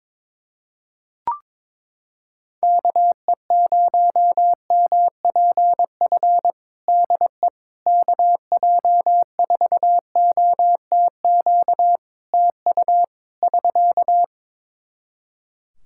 Morse code, heard as a keyed tone, about 22 wpm, 700 Hz. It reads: KE0MPF DE KJ4OTQ TU <SK>
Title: Practice QSO - Exchange 4 - Ending - 22wpm